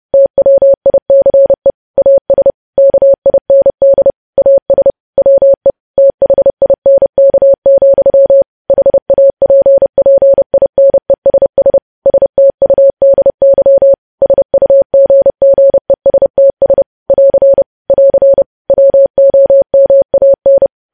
Morse code, heard as a keyed tone, about 30 wpm, 550 Hz. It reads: TWICE AS KIND AS WE THINK, HAPPINESS STUDY SUGGESTS <AR> <AR> WOMAN
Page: news headlines in morse code at 30 wpm